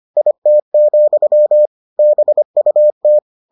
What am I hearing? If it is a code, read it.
Morse code, heard as a keyed tone, about 25 wpm, 600 Hz. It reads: IT, BUT